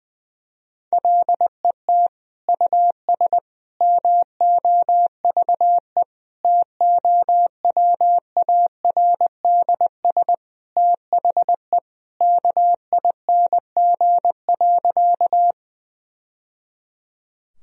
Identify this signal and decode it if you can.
Morse code, heard as a keyed tone, about 19 wpm, 700 Hz. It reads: LET US MOVE TOWARDS THE KING.
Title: Sentences from Top 1000 Words - Review Part 2 - 20wpm